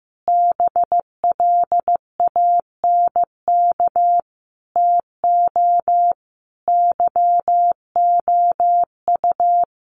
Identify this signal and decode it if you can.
Morse code, heard as a keyed tone, about 14 wpm, 700 Hz. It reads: BLANK TO YOU
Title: Sentences from Top 1500 Words 15wpm